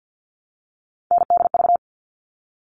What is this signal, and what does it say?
Morse code, heard as a keyed tone, about 50 wpm, 700 Hz. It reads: NDV
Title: All Cut Numbers - 3-Digit 50wpm